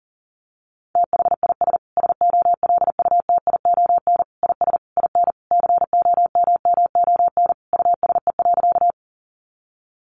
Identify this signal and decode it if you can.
Morse code, heard as a keyed tone, about 37 wpm, 700 Hz. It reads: THIS SOLUTION IS IN COMMON USE.
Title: Sentences from Top 1300 Words 40wpm